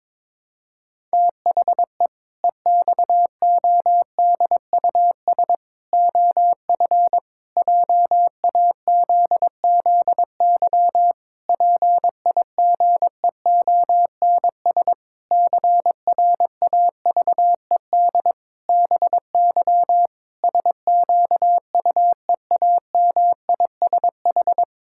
Morse code, heard as a keyed tone, about 21 wpm, 700 Hz. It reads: THE EXODUS OF JAZZY PIGEONS CRAVED BY SQUEAMISH
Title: Pangram Challenge v2 - 22wpm